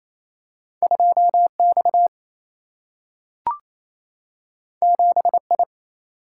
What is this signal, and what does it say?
Morse code, heard as a keyed tone, about 28 wpm, 700 Hz. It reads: 2X 7I
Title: Single-Letter Single-Number 28wpm